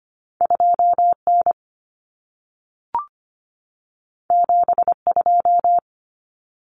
Morse code, heard as a keyed tone, about 25 wpm, 700 Hz. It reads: 2N 72
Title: Common Cut Numbers - 2-Digit 25wpm